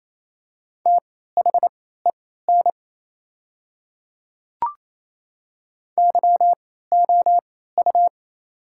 Morse code, heard as a keyed tone, about 28 wpm, spaced as for 16 wpm, 700 Hz. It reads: THEN YOU
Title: 3x Character Spacing - Top 100 Words - 28wpm